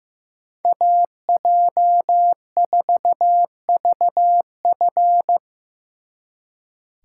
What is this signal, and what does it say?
Morse code, heard as a keyed tone, about 18 wpm, 700 Hz. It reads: AJ4VF
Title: Speed Racing - USA 2x2 Call Signs 15wpm